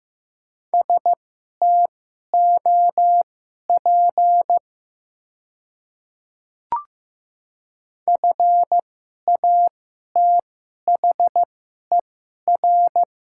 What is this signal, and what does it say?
Morse code, heard as a keyed tone, about 15 wpm, spaced as for 11 wpm, 700 Hz. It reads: STOP FATHER
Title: 2x Character Spacing - Top 300 Words - 15wpm